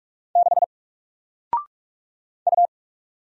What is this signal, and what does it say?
Morse code, heard as a keyed tone, about 44 wpm, 700 Hz. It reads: 6 U
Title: Single Letter-Number Alt 45wpm